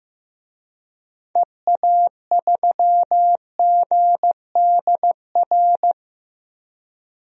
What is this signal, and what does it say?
Morse code, heard as a keyed tone, about 15 wpm, 700 Hz. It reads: EA3GDR
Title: DXCC-100 Call Signs - 6 Characters - Repeated 2x - 15wpm